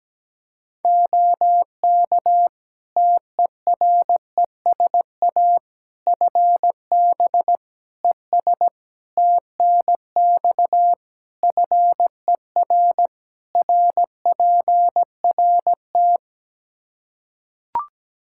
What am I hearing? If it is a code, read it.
Morse code, heard as a keyed tone, about 17 wpm, 700 Hz. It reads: OK TERESA FB ES TNX FER RPRT
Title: Practice QSO - Exchange 2 - Info - 17wpm